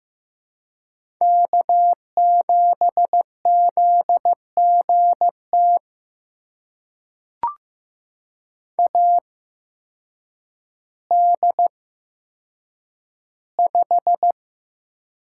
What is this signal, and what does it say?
Morse code, heard as a keyed tone, about 15 wpm, spaced as for 8 wpm, 700 Hz. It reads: K7ZGT AD5
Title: USA Call Signs - 8x Character Spacing - 15wpm